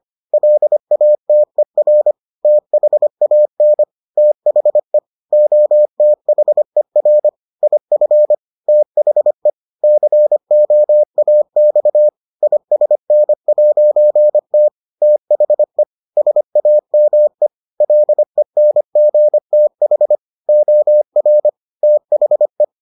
Morse code, heard as a keyed tone, about 25 wpm, 600 Hz. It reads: LATER THAN THE OTHER IF THE COAX ISN'T THE SAME LENGTH OR THE